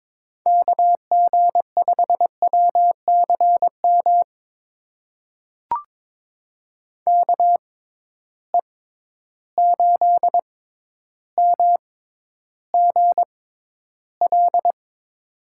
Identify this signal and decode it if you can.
Morse code, heard as a keyed tone, about 23 wpm, spaced as for 11 wpm, 700 Hz. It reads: KG5WCM KE8MGL
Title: USA Call Signs - 6 Characters - 6x Character Spacing - 22wpm